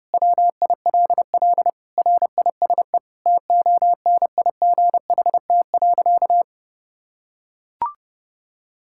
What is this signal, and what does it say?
Morse code, heard as a keyed tone, about 30 wpm, 700 Hz. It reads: WILL RISE TONIGHT.
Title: Sentences from Top 1000 Words - Review Part 6 - 30wpm